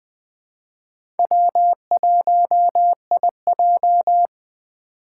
Morse code, heard as a keyed tone, about 21 wpm, 700 Hz. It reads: W1IJ
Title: USA Call Signs - 4 Characters - 4x Character Spacing - 20wpm